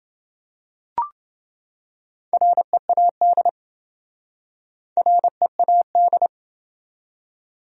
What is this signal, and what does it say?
Morse code, heard as a keyed tone, about 26 wpm, 700 Hz. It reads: READ READ